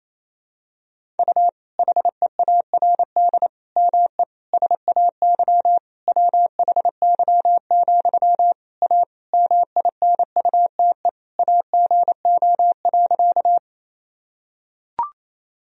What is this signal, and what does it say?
Morse code, heard as a keyed tone, about 27 wpm, 700 Hz. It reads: U HEARD ME SAY WHY, A MINUTE AGO.
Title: Sentences from Top 500 Words and Abbreviations 28wpm